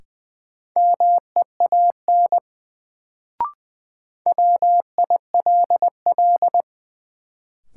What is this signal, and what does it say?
Morse code, heard as a keyed tone, about 20 wpm, 700 Hz. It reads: MEAN WILL
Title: Sets of 2 Words - New Word Drill - 20wpm